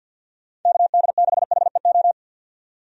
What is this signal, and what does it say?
Morse code, heard as a keyed tone, about 52 wpm, 700 Hz. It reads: KD6HEO
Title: USA Call Signs - 6 Characters - 50wpm